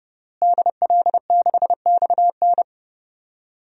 Morse code, heard as a keyed tone, about 32 wpm, 700 Hz. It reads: DL6XN
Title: DXCC-100 Call Signs - 5 Characters - Repeated 3x - 30wpm